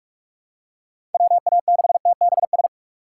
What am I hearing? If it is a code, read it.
Morse code, heard as a keyed tone, about 46 wpm, 700 Hz. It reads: WA6TBS
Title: USA Call Signs - 6 Characters - 4x Character Spacing - 45wpm